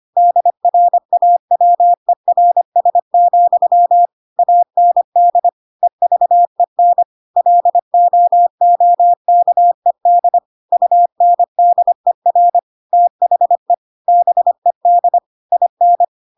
Morse code, heard as a keyed tone, about 25 wpm, 700 Hz. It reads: DRAWERS, AND EVEN LOOKED UNDER THE BED IN
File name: Pinocchio / Pinocchio ch5 25 25 ik1hge